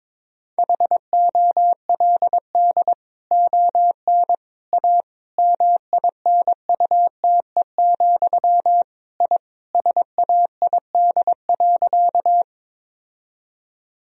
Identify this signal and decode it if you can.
Morse code, heard as a keyed tone, about 22 wpm, 700 Hz. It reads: HOLD ON A MINUTE, I SAID.
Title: Sentences from Top 1000 Words - Review Part 2 - 22wpm